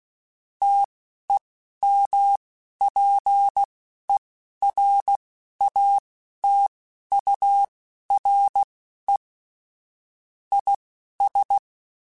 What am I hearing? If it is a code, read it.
Morse code, heard as a keyed tone, about 16 wpm, spaced as for 11 wpm, 800 Hz. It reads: TEMPERATURE IS
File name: Weather Sets / Weather Sets-mp3s / Weather 3 17 @10